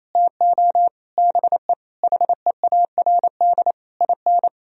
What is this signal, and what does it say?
Morse code, heard as a keyed tone, about 28 wpm, 700 Hz. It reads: TO BE HEARD IN
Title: Sentences from Top 1000 Words - Review Part 3 - 28wpm